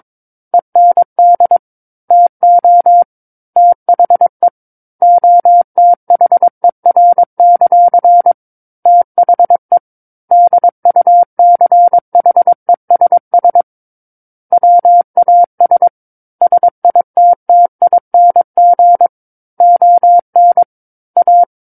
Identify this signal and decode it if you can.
Morse code, heard as a keyed tone, about 22 wpm, 700 Hz. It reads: END TO THE OTHER; THE DUCHESS WAS SITTING ON A
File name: 06-Chapter-06 22 WPM